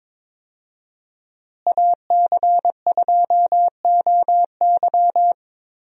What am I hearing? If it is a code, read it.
Morse code, heard as a keyed tone, about 22 wpm, 700 Hz. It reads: AC2OY
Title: USA Call Signs - 5 Characters - Repeated 2x - 22wpm